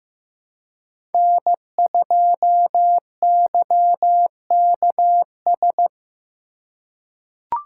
Morse code, heard as a keyed tone, about 16 wpm, 700 Hz. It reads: N2YKS
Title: USA Call Signs - 5 Characters - 2x Character Spacing - 15wpm